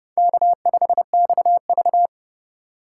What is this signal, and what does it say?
Morse code, heard as a keyed tone, about 31 wpm, 700 Hz. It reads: K5XV
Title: USA Call Signs - 4 Characters - 2x Character Spacing - 30wpm